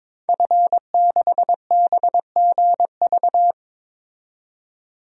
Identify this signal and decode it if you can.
Morse code, heard as a keyed tone, about 22 wpm, 700 Hz. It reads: F6BGV
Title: DXCC-100 Call Signs - 5 Characters - 22wpm